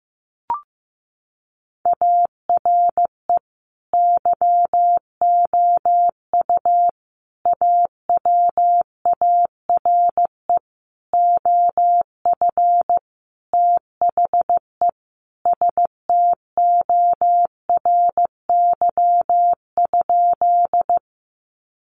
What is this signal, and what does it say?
Morse code, heard as a keyed tone, about 15 wpm, 700 Hz. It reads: ARE YOU AWARE OF THE STORY?
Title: Sentences from Top 1200 Words 15wpm